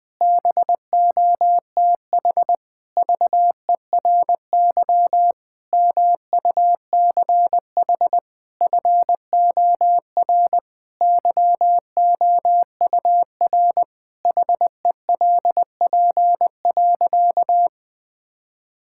Morse code, heard as a keyed tone, about 21 wpm, 700 Hz. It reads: BOTH VERY MUCH FOR YOUR HELP.
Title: Sentences with 5-Word Phrases - 20wpm